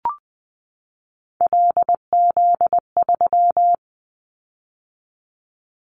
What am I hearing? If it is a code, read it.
Morse code, heard as a keyed tone, about 20 wpm, 700 Hz. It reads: LZ3
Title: Callsign Prefixes for Top 100 DXCC - 20wpm